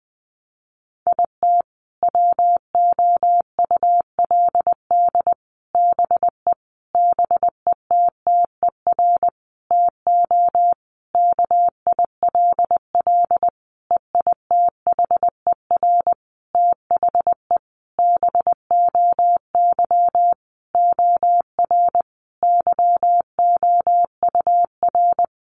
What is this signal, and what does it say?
Morse code, heard as a keyed tone, about 20 wpm, 700 Hz. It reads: IT WOULD BE BETTER TO KILL EITHER THE BOY OR YOUR
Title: Sentences from Top 1500 Words 20wpm